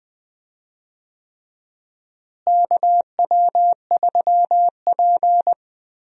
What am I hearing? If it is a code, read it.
Morse code, heard as a keyed tone, about 20 wpm, 700 Hz. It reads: KW3P
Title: USA Call Signs - 4 Characters - Repeated 3x - 20wpm